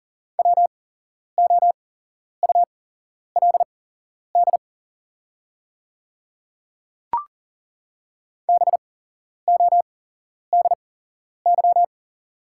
Morse code, heard as a keyed tone, about 40 wpm, spaced as for 11 wpm, 700 Hz. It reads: WOULD BODY
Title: 8x Character Spacing - Top 500 Words - 40wpm